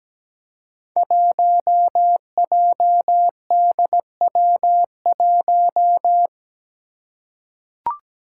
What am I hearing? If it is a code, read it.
Morse code, heard as a keyed tone, about 17 wpm, 700 Hz. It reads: 1JDW1
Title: Five Challenging Letter-Number Combinations 17wpm